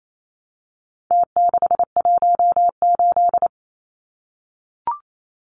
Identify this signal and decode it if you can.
Morse code, heard as a keyed tone, about 28 wpm, 700 Hz. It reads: T618
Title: Numbers - Contest Style - 4-Digit 28wpm